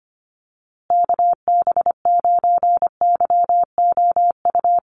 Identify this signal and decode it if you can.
Morse code, heard as a keyed tone, about 25 wpm, 700 Hz. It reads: KB9YOU
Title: USA Call Signs Repeated 2x 25wpm